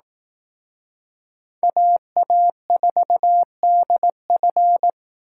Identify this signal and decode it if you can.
Morse code, heard as a keyed tone, about 18 wpm, 700 Hz. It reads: AA4DF
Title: Speed Racing - USA 2x2 Call Signs 15wpm